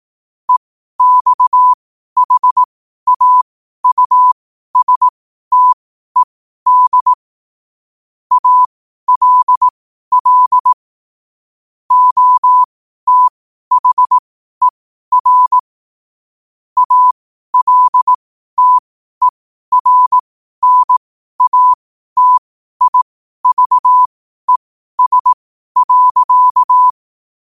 Morse code, heard as a keyed tone, about 18 wpm, spaced as for 12 wpm, 1000 Hz. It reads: EXHAUSTED ALL OTHER ALTERNATIVES.
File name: qotdfiles / 12wpm